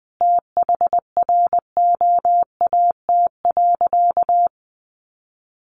Morse code, heard as a keyed tone, about 20 wpm, 700 Hz. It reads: THROAT.
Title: A Sound Of Thunder 20wpm